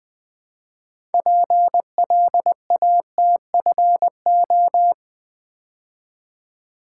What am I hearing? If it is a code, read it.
Morse code, heard as a keyed tone, about 20 wpm, 700 Hz. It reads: PLATFO
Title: Build A Word - 3 Letters At A Time - 8-Letter Words - 20wpm